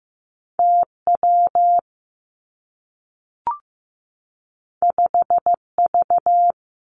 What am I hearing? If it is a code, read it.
Morse code, heard as a keyed tone, about 15 wpm, 700 Hz. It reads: T W 5 V
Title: Two Challenging Letter-Number Combinations 15wpm